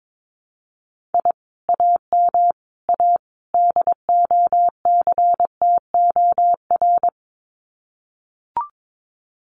Morse code, heard as a keyed tone, about 22 wpm, 700 Hz. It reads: I AM A DOCTOR
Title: Common QSO Phrases Encore 22wpm